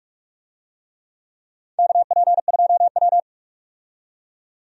Morse code, heard as a keyed tone, about 46 wpm, 700 Hz. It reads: KP2W
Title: USA Call Signs - 4 Characters - 45wpm